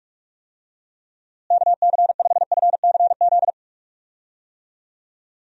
Morse code, heard as a keyed tone, about 45 wpm, 700 Hz. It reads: KC5FCZ
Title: USA Call Signs - 6 Characters - Repeated 2x - 45wpm